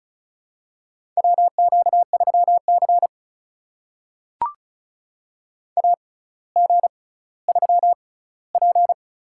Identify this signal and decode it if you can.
Morse code, heard as a keyed tone, about 36 wpm, spaced as for 18 wpm, 700 Hz. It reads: WQ3C AG3P
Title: USA Call Signs - 4 Characters - 6x Character Spacing - 35wpm